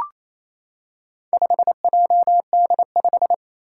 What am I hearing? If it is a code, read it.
Morse code, heard as a keyed tone, about 27 wpm, 700 Hz. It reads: 5JD5
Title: Four Challenging Letter-Number Combinations 28wpm